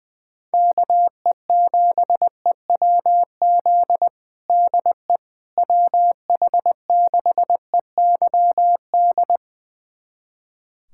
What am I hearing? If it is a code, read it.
Morse code, heard as a keyed tone, about 20 wpm, 700 Hz. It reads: KE7EWZ DE WH6EYD
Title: USA Callsign DE Callsign 20wpm